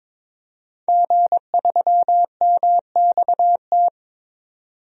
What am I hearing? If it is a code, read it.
Morse code, heard as a keyed tone, about 22 wpm, 700 Hz. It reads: G3MXT
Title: DXCC-100 Call Signs - 5 Characters - 22wpm